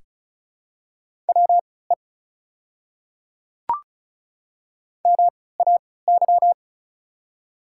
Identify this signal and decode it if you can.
Morse code, heard as a keyed tone, about 36 wpm, spaced as for 20 wpm, 700 Hz. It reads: WE MAY
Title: 3x Character Spacing - Top 200 Words - 35wpm